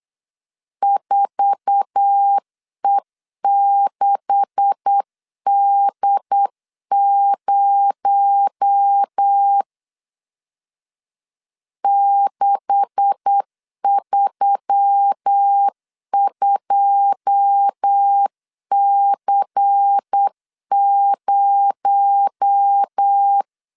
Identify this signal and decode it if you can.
Morse code, heard as a keyed tone, about 8 wpm, 800 Hz. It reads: 4E6D0 632C0
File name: LEZIONI / 9 286 8wpm 3425